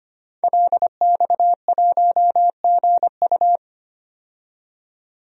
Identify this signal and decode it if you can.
Morse code, heard as a keyed tone, about 25 wpm, 700 Hz. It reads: LX1GU
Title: DXCC-100 Call Signs - 5 Characters - 25wpm